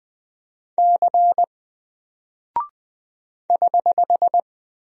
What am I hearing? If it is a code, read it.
Morse code, heard as a keyed tone, about 20 wpm, 700 Hz. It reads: C <HH>
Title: Single Character (Letter:Number:Punctuation:Prosign) Alt 20wpm